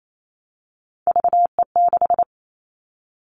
Morse code, heard as a keyed tone, about 28 wpm, 700 Hz. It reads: VE6
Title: Callsign Prefixes for Top 50 DXCC - 28wpm